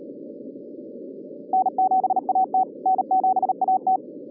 Morse code, heard as a keyed tone, about 38 wpm, 760 Hz. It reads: N7AT N7AT